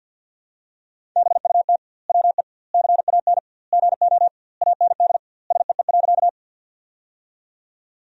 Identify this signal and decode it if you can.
Morse code, heard as a keyed tone, about 50 wpm, 700 Hz. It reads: BUT WE CAN GO AND SEE.
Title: 2x Word-Spacing - Sentences from Top 100 Words - 50wpm